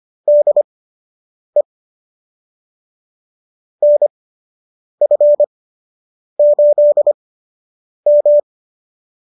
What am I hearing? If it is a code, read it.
Morse code, heard as a keyed tone, about 25 wpm, spaced as for 8 wpm, 590 Hz. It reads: DE NF8M